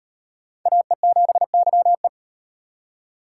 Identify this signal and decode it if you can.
Morse code, heard as a keyed tone, about 38 wpm, 700 Hz. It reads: AE7YE